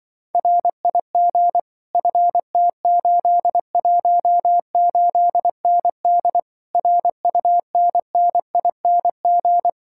Morse code, heard as a keyed tone, about 24 wpm, 700 Hz. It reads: RIG FT818ND RUNNING
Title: Speed-Racing Common QSO Phrases - 20wpm